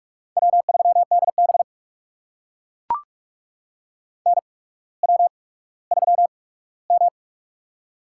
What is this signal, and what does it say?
Morse code, heard as a keyed tone, about 44 wpm, spaced as for 19 wpm, 700 Hz. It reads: W3DB NW3M
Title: USA Call Signs - 4 Characters - 8x Character Spacing - 45wpm